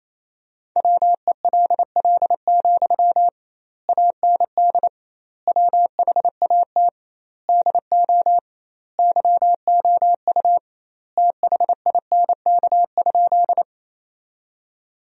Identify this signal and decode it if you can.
Morse code, heard as a keyed tone, about 28 wpm, 700 Hz. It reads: WELL, AND WHAT DO YOU THINK?
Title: 2x Word-Spacing - Sentences from Top 200 Words - 28wpm